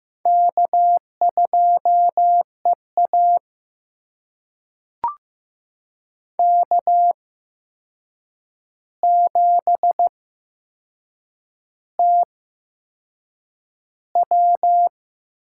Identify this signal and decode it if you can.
Morse code, heard as a keyed tone, about 15 wpm, spaced as for 6 wpm, 700 Hz. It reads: K2EA K7TW
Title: USA Call Signs - 4 Characters - 8x Character Spacing - 15wpm